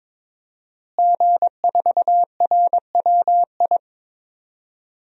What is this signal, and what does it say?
Morse code, heard as a keyed tone, about 22 wpm, 700 Hz. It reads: G4RWI
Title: DXCC-100 Call Signs - 5 Characters - Repeated 2x - 22wpm